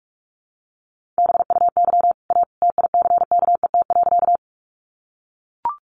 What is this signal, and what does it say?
Morse code, heard as a keyed tone, about 44 wpm, 700 Hz. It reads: BUY A TICKET.
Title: Sentences from Top 1400 Words 45wpm